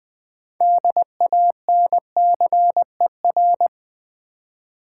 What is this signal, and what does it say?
Morse code, heard as a keyed tone, about 20 wpm, 700 Hz. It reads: DANCER
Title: 6 Letter Words - First 3 Letters Given - 20wpm